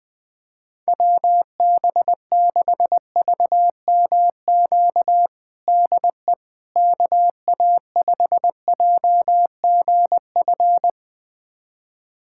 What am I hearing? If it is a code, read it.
Morse code, heard as a keyed tone, about 21 wpm, 700 Hz. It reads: WB6VMQ DE KA5JGF
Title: USA Callsign DE Callsign - 6 Characters - 20wpm